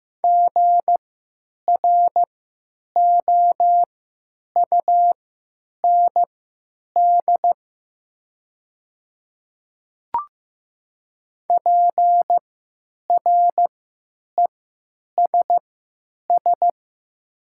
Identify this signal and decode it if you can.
Morse code, heard as a keyed tone, about 15 wpm, spaced as for 9 wpm, 700 Hz. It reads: GROUND PRESS
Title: 3x Character Spacing - Top 500 Words - 15wpm